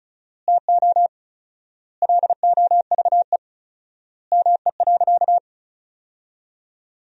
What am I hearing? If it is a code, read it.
Morse code, heard as a keyed tone, about 36 wpm, 700 Hz. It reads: TO LOVE ME.
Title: 4x Word-Spacing - Sentences from Top 400 Words - 35wpm